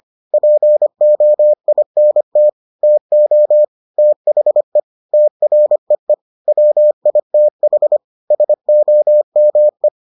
Morse code, heard as a keyed tone, about 25 wpm, 600 Hz. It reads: POINT TO THE TREE WITH SOME